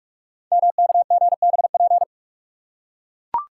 Morse code, heard as a keyed tone, about 44 wpm, 700 Hz. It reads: MKGBP
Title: Code Groups - 5 Letters - 45wpm